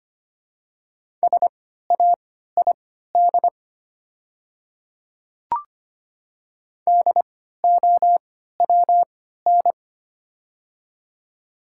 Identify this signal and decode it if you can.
Morse code, heard as a keyed tone, about 25 wpm, spaced as for 14 wpm, 700 Hz. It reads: SAID DOWN
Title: 3x Character Spacing - Top 100 Words - 25wpm